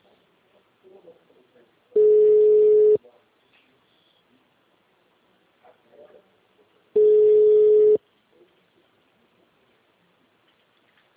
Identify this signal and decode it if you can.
Ringback tone (Europe, 425 Hz), with speech.